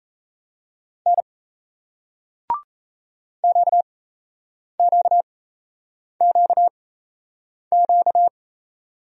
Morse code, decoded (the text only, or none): N QQQQ